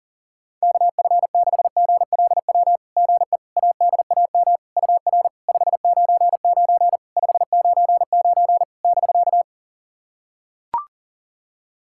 Morse code, heard as a keyed tone, about 40 wpm, 690 Hz. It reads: KF6GLW GE ADAM UR 599 599 <BK>
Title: SOTA QSO Practice 40wpm